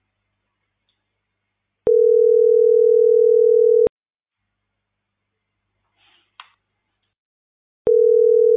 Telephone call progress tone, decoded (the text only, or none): ringback tone